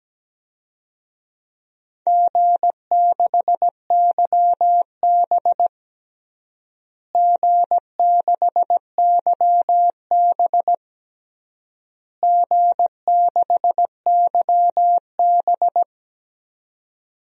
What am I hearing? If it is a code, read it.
Morse code, heard as a keyed tone, about 17 wpm, 700 Hz. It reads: G6YB G6YB G6YB
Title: DXCC-100 Call Signs - 4 Characters - Repeated 3x - 17wpm